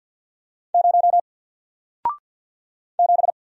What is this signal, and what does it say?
Morse code, heard as a keyed tone, about 50 wpm, 700 Hz. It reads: 0 7
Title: Single Letter-Number 50wpm